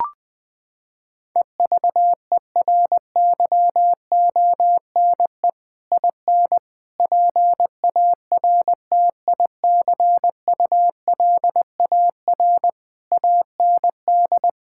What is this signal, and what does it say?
Morse code, heard as a keyed tone, about 21 wpm, 700 Hz. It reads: EVERYONE IN PARTICULAR AND
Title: Sentences from Top 1600 Words 20wpm